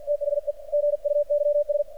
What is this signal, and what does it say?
Morse code, heard as a keyed tone, about 45 wpm, 590 Hz. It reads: THE GUYS
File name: archived / CRSnet